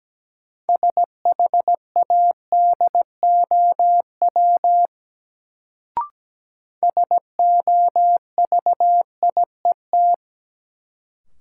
Morse code, heard as a keyed tone, about 17 wpm, 700 Hz. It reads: SHADOW SOVIET
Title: Sets of 5 Words Encore - New Word Drill - 17wpm